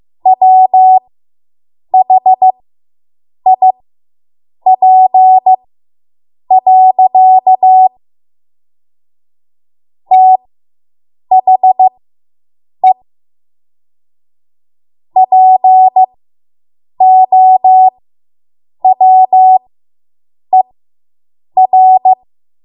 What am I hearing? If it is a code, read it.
Morse code, heard as a keyed tone, about 15 wpm, spaced as for 7 wpm, 750 Hz. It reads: WHIP. THE POWER